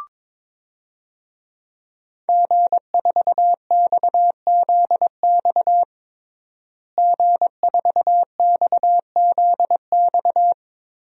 Morse code, heard as a keyed tone, about 22 wpm, 700 Hz. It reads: G4XZX G4XZX
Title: DXCC-100 Call Signs - 5 Characters - Repeated 3x - 22wpm